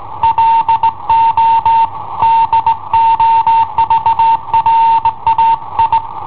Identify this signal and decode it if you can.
Morse code, heard as a keyed tone, about 18 wpm, 910 Hz. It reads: LO DOVRAI
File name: i7ohp / cassetta i7ohp B